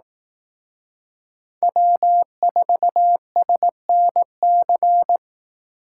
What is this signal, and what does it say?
Morse code, heard as a keyed tone, about 18 wpm, 700 Hz. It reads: W4SNC
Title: Speed Racing - USA 1x3 Call Signs 15wpm